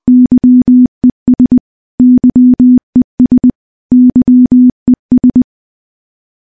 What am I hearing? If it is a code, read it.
Morse code, heard as a keyed tone, about 21 wpm, 250 Hz. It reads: YES YES YES
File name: morse-training-20WPM